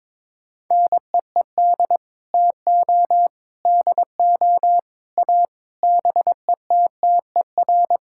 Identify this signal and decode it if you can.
Morse code, heard as a keyed tone, about 22 wpm, 700 Hz. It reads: NEED TO DO A BETTER
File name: practice / 22wpm / Sets of 7 Words - 22wpm